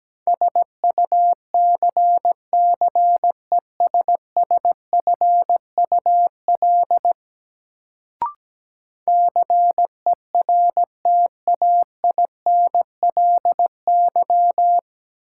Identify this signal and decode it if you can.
Morse code, heard as a keyed tone, about 17 wpm, 700 Hz. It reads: SUCCESSFUL CERTAINLY
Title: Sentences from Top 900 Words - New Word Drill - 17wpm